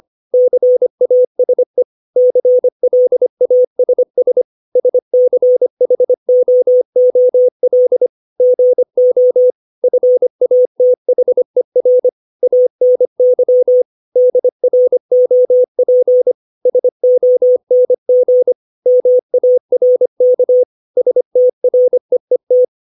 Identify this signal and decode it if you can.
Morse code, heard as a keyed tone, about 25 wpm, 500 Hz. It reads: CASE CLASS SCHOOL GO FATHER ANY DROP SONG MARK STREET
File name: 632 words / 632 Words 25WPM